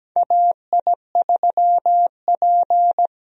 Morse code, heard as a keyed tone, about 17 wpm, 700 Hz. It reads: AI3P
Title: USA Call Signs - 4 Characters - Repeated 3x - 17wpm